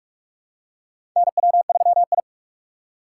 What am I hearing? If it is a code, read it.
Morse code, heard as a keyed tone, about 46 wpm, 700 Hz. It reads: NW3I
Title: USA Call Signs - 4x Character Spacing - 45wpm